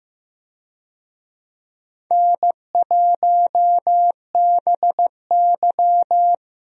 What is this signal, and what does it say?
Morse code, heard as a keyed tone, about 15 wpm, 700 Hz. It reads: N1BY
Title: USA Call Signs - 4 Characters - Repeated 2x - 15wpm